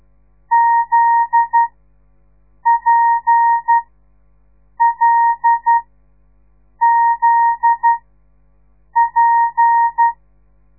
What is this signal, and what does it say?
Morse code, heard as a keyed tone, about 12 wpm, spaced as for 6 wpm, 930 Hz. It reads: ZPLZP
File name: radiomorningstar / morse / les 8